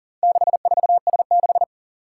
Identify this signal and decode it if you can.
Morse code, heard as a keyed tone, about 44 wpm, 700 Hz. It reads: 64S6
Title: Four Challenging Letter-Number Combinations 40wpm